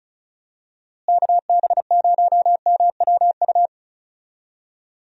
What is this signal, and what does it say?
Morse code, heard as a keyed tone, about 36 wpm, 700 Hz. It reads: KB0MWU
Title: USA Call Signs - 6 Characters - Repeated 3x - 35wpm